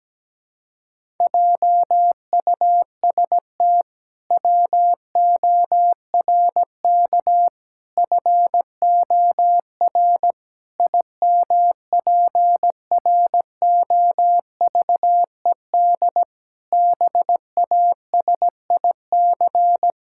Morse code, heard as a keyed tone, about 17 wpm, 700 Hz. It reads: JUST WORK FOR IMPROVED BASIC
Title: Pangram Challenge v2 - 17wpm